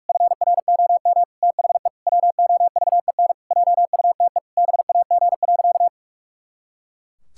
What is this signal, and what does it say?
Morse code, heard as a keyed tone, about 46 wpm, 700 Hz. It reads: FROM THE WOVEN JUTE BAG.
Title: Pangram Challenge v2 - 45wpm